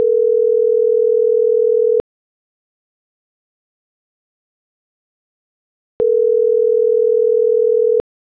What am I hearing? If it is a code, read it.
Ringback tone (North America), 440 + 480 Hz.